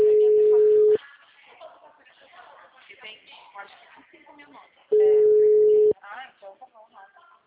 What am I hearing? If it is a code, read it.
Ringback tone (Europe, 425 Hz), with speech.
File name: out- -2012- - - .508